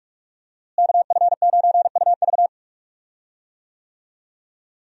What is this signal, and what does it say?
Morse code, heard as a keyed tone, about 43 wpm, 700 Hz. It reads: KF9UV
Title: USA Call Signs - 5 Characters - Repeated 3x - 45wpm